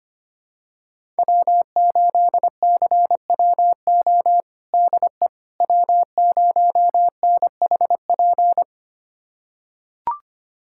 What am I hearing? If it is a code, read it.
Morse code, heard as a keyed tone, about 25 wpm, 700 Hz. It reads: W8CWO DE W0NHP
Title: USA Callsign DE Callsign - 5 Characters - 25wpm